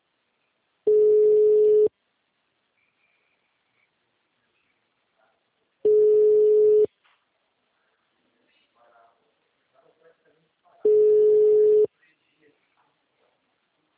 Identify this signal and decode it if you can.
Ringback tone (Europe, 425 Hz), with speech.
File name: out- -2001- - - .3508